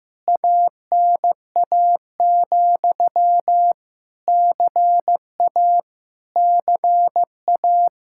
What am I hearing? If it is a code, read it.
Morse code, heard as a keyed tone, about 15 wpm, 700 Hz. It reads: ANA, CA CA
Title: Practice QSO - Exchange 1 - Beginning - 15wpm